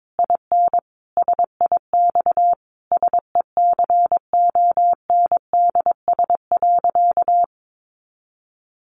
Morse code, heard as a keyed tone, about 22 wpm, 700 Hz. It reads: IN SIX SECONDS.
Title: A Sound Of Thunder 22wpm